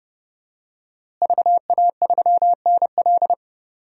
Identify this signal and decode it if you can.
Morse code, heard as a keyed tone, about 30 wpm, 700 Hz. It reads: VA3NL
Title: DXCC-100 Call Signs - 5 Characters - Repeated 3x - 30wpm